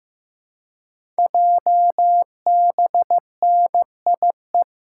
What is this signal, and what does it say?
Morse code, heard as a keyed tone, about 15 wpm, 700 Hz. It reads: JBNIE
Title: Code Groups - 5 Letters - 15wpm